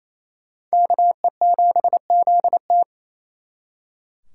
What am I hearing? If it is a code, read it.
Morse code, heard as a keyed tone, about 35 wpm, 700 Hz. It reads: KE7ZT